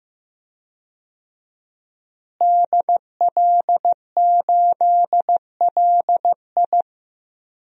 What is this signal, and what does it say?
Morse code, heard as a keyed tone, about 15 wpm, 700 Hz. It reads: DL8LI
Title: DXCC-100 Call Signs - 5 Characters - Repeated 3x - 15wpm